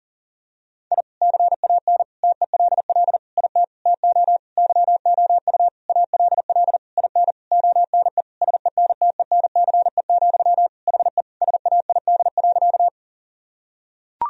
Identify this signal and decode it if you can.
Morse code, heard as a keyed tone, about 40 wpm, 700 Hz. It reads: I CAN TELL IT TO YOU ALL IN ONE SENTENCE, HE SAID.
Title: Sentences from Top 1000 Words - Review Part 1 - 40wpm